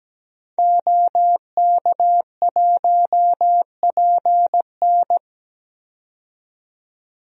Morse code, heard as a keyed tone, about 17 wpm, 700 Hz. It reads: OK1PN
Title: DXCC-100 Call Signs - 5 Characters - 17wpm